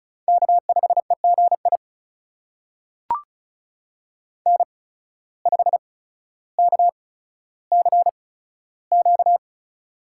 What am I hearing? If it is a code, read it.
Morse code, heard as a keyed tone, about 36 wpm, spaced as for 15 wpm, 700 Hz. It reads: K5EGI N5KCQ
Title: USA Call Signs - 5 Characters - 8x Character Spacing - 35wpm